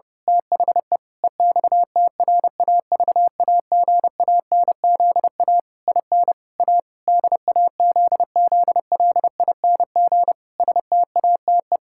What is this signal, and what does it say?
Morse code, heard as a keyed tone, about 30 wpm, 700 Hz. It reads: THE EXTRAVAGANZA IN A DAZZLING STATE
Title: Pangram Challenge v2 - 30wpm